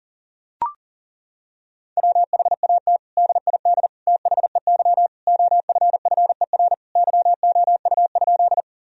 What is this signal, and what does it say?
Morse code, heard as a keyed tone, about 37 wpm, 700 Hz. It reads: WHAT DID THEY OFFER YOU?
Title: Sentences from Top 1000 Words - Review Part 4 - 40wpm